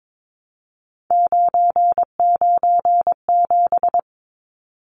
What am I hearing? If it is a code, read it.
Morse code, heard as a keyed tone, about 22 wpm, 700 Hz. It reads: 997
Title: Numbers - 3-Digit 22wpm